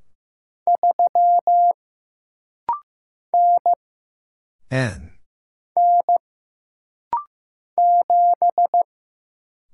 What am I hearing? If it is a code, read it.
Morse code, heard as a keyed tone, about 15 wpm, 700 Hz. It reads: 3 N N 7
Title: Single Letter-Number Alt - Rapid-Fire 15wpm